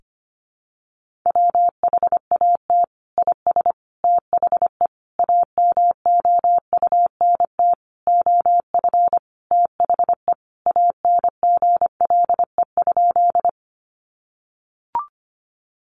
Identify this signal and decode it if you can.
Morse code, heard as a keyed tone, about 25 wpm, 700 Hz. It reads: WHAT IS THE AMOUNT OF THE ANGLE?
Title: Sentences from Top 1500 Words 25wpm